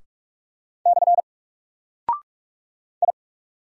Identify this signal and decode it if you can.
Morse code, heard as a keyed tone, about 44 wpm, 700 Hz. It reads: / I
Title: Single Character (Letter:Number:Punctuation:Prosign) 45wpm